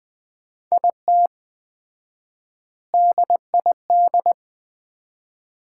Morse code, heard as a keyed tone, about 20 wpm, 700 Hz. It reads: IT DID
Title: 4x Word-Spacing - Sets of 3 Words - 20wpm